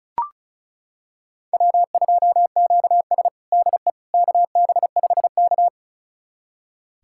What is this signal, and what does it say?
Morse code, heard as a keyed tone, about 36 wpm, 700 Hz. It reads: W2QS DE KB5K
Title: USA Callsign DE Callsign - 4 Characters - 35wpm